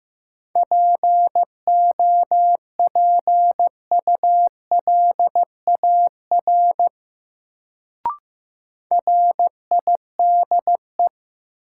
Morse code, heard as a keyed tone, about 15 wpm, 700 Hz. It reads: POPULAR RIDE
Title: Sentences from Top 1200 Words - New Word Drill Alt - 15wpm